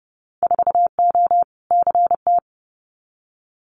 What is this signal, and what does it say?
Morse code, heard as a keyed tone, about 30 wpm, 700 Hz. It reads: 4O CT
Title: Winter Field Day Exchanges Encore 30wpm